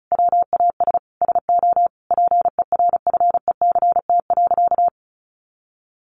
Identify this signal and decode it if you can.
Morse code, heard as a keyed tone, about 34 wpm, 700 Hz. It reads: WAS SO PERFECT.
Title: Sentences from Top 1300 Words 35wpm